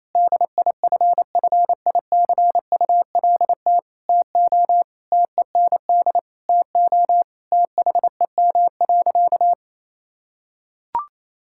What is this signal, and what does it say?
Morse code, heard as a keyed tone, about 28 wpm, 700 Hz. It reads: DIFFICULT TO TEND TO THEM.
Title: Sentences from Top 1000 Words - Review Part 5 - 28wpm